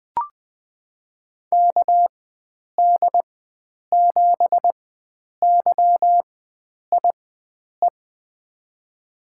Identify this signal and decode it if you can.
Morse code, heard as a keyed tone, about 20 wpm, spaced as for 9 wpm, 700 Hz. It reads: KD7YIE
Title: USA Call Signs - 6 Characters - 4x Character Spacing - 20wpm